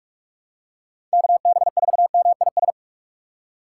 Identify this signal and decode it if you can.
Morse code, heard as a keyed tone, about 46 wpm, 700 Hz. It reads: KB4MIS
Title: USA Call Signs - 3x Character Spacing - 45wpm